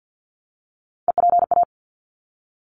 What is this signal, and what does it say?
Morse code, heard as a keyed tone, about 49 wpm, 700 Hz. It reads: EPA
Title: ARRL Sections 50wpm